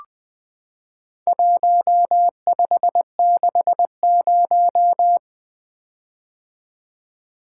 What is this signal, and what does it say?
Morse code, heard as a keyed tone, about 20 wpm, 700 Hz. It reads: 1560
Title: Four Letter-Number Combinations - 20wpm